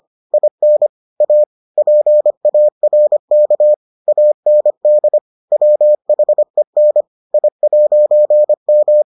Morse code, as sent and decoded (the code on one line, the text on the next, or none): .. -. / .- / .--. .- .-. -.- / .- -. -.. / .-- .... -. / .. .----. --
IN A PARK AND WHEN I'M